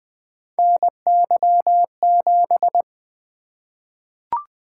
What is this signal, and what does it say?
Morse code, heard as a keyed tone, about 20 wpm, 700 Hz. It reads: NY7
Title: Three Letter-Number Combinations - 20wpm